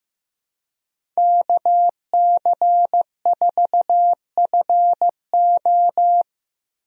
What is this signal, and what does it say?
Morse code, heard as a keyed tone, about 15 wpm, 700 Hz. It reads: KC4FO
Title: USA Call Signs - 5 Characters - 15wpm